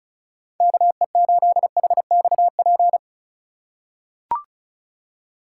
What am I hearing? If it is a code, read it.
Morse code, heard as a keyed tone, about 36 wpm, 700 Hz. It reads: KE8HXP
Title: USA Call Signs - 6 Characters - 4x Character Spacing - 35wpm